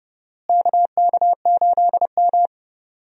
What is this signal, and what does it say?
Morse code, heard as a keyed tone, about 30 wpm, 700 Hz. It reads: KK8M
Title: USA Call Signs - 4 Characters - Repeated 3x - 30wpm